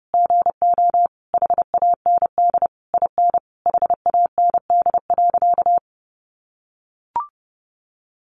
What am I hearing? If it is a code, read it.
Morse code, heard as a keyed tone, about 28 wpm, 700 Hz. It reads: GO HAND IN HAND.
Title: Sentences from Top 1200 Words 30wpm